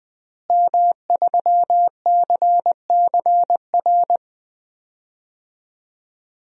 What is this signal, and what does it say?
Morse code, heard as a keyed tone, about 20 wpm, 700 Hz. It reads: M3CCR
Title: DXCC-100 Call Signs - 5 Characters - Repeated 2x - 20wpm